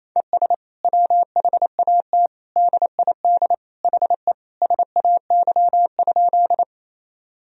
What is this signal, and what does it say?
Morse code, heard as a keyed tone, about 28 wpm, 700 Hz. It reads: ES WHAT DID HE SAY?
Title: Sentences from Top 200 Words and Abbreviations 28wpm